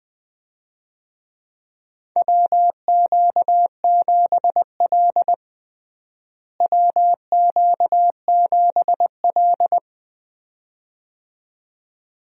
Morse code, heard as a keyed tone, about 20 wpm, 700 Hz. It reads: WQ7L WQ7L
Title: USA Call Signs - 4 Characters - Repeated 2x - 20wpm